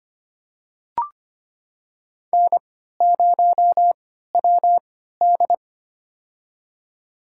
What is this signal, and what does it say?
Morse code, heard as a keyed tone, about 25 wpm, spaced as for 14 wpm, 700 Hz. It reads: N0WD
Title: USA Call Signs - 4 Characters - 3x Character Spacing - 25wpm